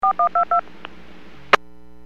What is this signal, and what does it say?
DTMF, heard as telephone touch tones, tones of about 110 ms. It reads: 4122